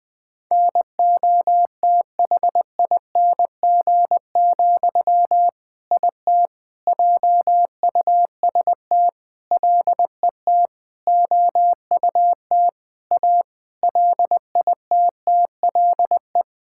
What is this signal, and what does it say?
Morse code, heard as a keyed tone, about 20 wpm, 700 Hz. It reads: NOTHING, IT JUST LET OUT A LITTLE
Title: All-Age Appropriate Jokes - 20wpm